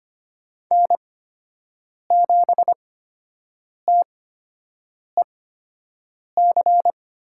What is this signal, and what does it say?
Morse code, heard as a keyed tone, about 25 wpm, spaced as for 7 wpm, 700 Hz. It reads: N7TEC
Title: USA Call Signs - 8x Character Spacing - 25wpm